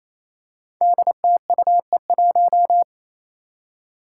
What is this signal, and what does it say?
Morse code, heard as a keyed tone, about 28 wpm, 700 Hz. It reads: DTUE1
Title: Five Challenging Letter-Number Combinations 28wpm